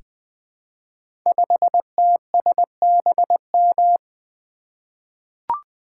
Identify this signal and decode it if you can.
Morse code, heard as a keyed tone, about 20 wpm, 700 Hz. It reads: 5TSBM
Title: Code Groups - 5 Letter:Numbers - 20wpm